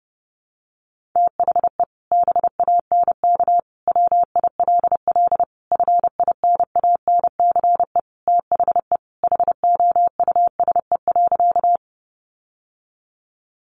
Morse code, heard as a keyed tone, about 30 wpm, 700 Hz. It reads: THE BANK WILL FINANCE THE HOUSE.
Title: Sentences from Top 1500 Words 30wpm